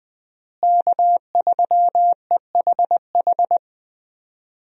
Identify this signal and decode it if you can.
Morse code, heard as a keyed tone, about 20 wpm, 700 Hz. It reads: K3EHH
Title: USA Call Signs - 5 Characters - Repeated 3x - 20wpm